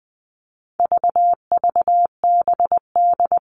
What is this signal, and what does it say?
Morse code, heard as a keyed tone, about 20 wpm, 700 Hz. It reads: VVBD
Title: All Cut Numbers - 4-Digit 20wpm